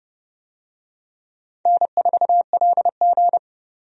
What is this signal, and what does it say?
Morse code, heard as a keyed tone, about 30 wpm, 700 Hz. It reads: N4LG
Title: USA Call Signs - 4 Characters - 30wpm